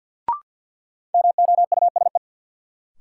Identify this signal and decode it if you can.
Morse code, heard as a keyed tone, about 53 wpm, 700 Hz. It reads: MOUSE
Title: Sentences from Top 700 Words - New Word Drill - 50wpm